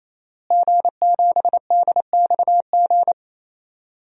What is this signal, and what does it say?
Morse code, heard as a keyed tone, about 28 wpm, 700 Hz. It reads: G7DXG
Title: DXCC-100 Call Signs - 5 Characters - Repeated 3x - 28wpm